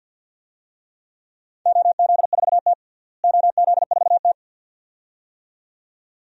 Morse code, heard as a keyed tone, about 51 wpm, 700 Hz. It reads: OZ4T OZ4T
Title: DXCC-100 Call Signs - 4 Characters - Repeated 2x - 50wpm